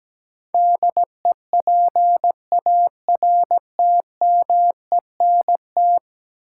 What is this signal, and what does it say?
Morse code, heard as a keyed tone, about 17 wpm, 700 Hz. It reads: DEPARTMENT
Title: Sentences from Top 1200 Words - New Word Drill Alt - 17wpm